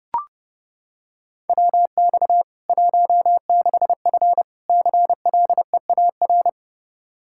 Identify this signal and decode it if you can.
Morse code, heard as a keyed tone, about 32 wpm, 700 Hz. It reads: WX 16F CLEAR
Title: Practice QSO - Exchange 2 - Info - 30wpm